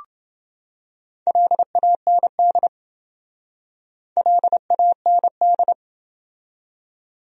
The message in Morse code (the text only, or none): LAND LAND